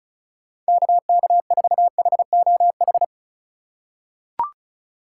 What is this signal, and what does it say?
Morse code, heard as a keyed tone, about 36 wpm, 690 Hz. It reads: KK4HOH
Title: USA Call Signs - 6 Characters - 4x Character Spacing - 35wpm